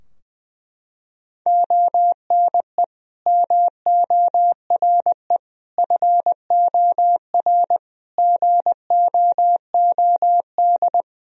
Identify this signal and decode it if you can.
Morse code, heard as a keyed tone, about 20 wpm, 700 Hz. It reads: ONE MORE FOR GOOD
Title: Sentences from Top 1000 Words - Review Part 2 - 20wpm